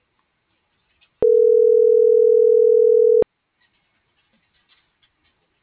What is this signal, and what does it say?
Ringback tone (North America), 440 + 480 Hz.
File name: CENTRAL GENERAL / out alexia.jauregui